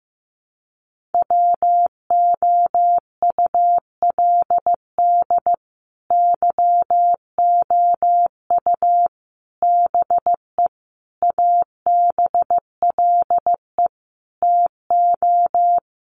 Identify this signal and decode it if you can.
Morse code, heard as a keyed tone, about 15 wpm, 700 Hz. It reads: WOULD YOU BE ABLE TO
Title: Sentences from Top 1400 Words 15wpm